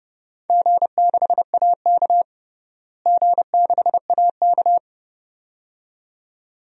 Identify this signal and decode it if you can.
Morse code, heard as a keyed tone, about 30 wpm, 700 Hz. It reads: G6AK G6AK
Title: DXCC-100 Call Signs - 4 Characters - Repeated 2x - 30wpm